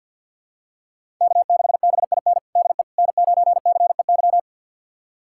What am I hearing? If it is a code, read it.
Morse code, heard as a keyed tone, about 52 wpm, 700 Hz. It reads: K6BIN DE N9CEY
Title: USA Callsign DE Callsign - 5 Characters - 50wpm